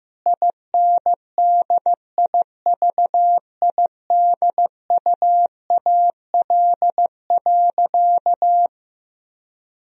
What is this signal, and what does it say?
Morse code, heard as a keyed tone, about 15 wpm, 700 Hz. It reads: INDIVIDUAL.
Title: Sentences from Top 1000 Words - Review Part 4 - 15wpm